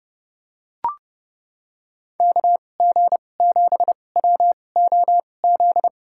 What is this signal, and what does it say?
Morse code, heard as a keyed tone, about 30 wpm, spaced as for 22 wpm, 700 Hz. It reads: KG7WOZ
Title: USA Call Signs - 2x Character Spacing - 30wpm